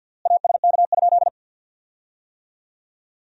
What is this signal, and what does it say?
Morse code, heard as a keyed tone, about 50 wpm, 700 Hz. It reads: ASK?
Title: 3x Word-Spacing - Sentences from Top 200 Words - 50wpm